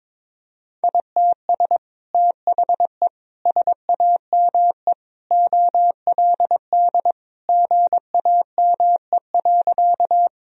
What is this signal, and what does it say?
Morse code, heard as a keyed tone, about 22 wpm, 700 Hz. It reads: ITS THE SAME OLD GAME.
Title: Sentences from Top 1000 Words - Review Part 3 - 22wpm